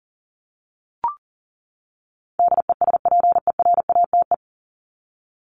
Morse code, heard as a keyed tone, about 40 wpm, 700 Hz. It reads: DESPERATE
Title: Top 1751-2000 Words 40wpm